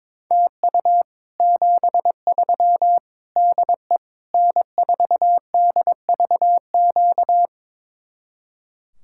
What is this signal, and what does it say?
Morse code, heard as a keyed tone, about 23 wpm, 700 Hz. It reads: TU 73 DE N4DVQ
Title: Speed-Racing Common QSO Phrases Encore - 22wpm